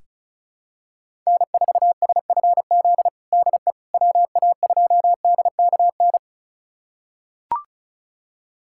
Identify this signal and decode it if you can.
Morse code, heard as a keyed tone, about 36 wpm, 700 Hz. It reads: N4SFZ DE WA2DKN
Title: USA Callsign DE Callsign 35wpm